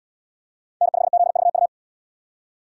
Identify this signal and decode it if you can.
Morse code, heard as a keyed tone, about 69 wpm, 700 Hz. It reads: N5ZVU